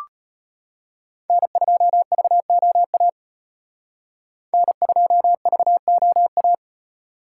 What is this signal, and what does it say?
Morse code, heard as a keyed tone, about 32 wpm, 700 Hz. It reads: N2VOA N2VOA